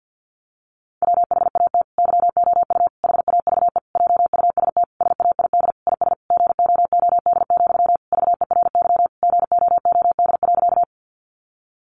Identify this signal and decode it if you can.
Morse code, heard as a keyed tone, about 50 wpm, 700 Hz. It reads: WHAT YOU HAVE JUST SAID IS GOOD, VERY GOOD.
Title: Sentences from Top 200 Words 50wpm